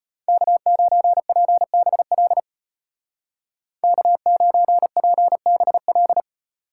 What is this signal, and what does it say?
Morse code, heard as a keyed tone, about 31 wpm, 700 Hz. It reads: K9PBL K9PBL
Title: Speed Racing - USA 1x3 Call Signs 25wpm